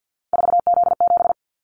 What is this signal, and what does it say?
Morse code, heard as a keyed tone, about 50 wpm, 700 Hz. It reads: 4Z7
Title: Callsign Prefixes for Top 50 DXCC - 50wpm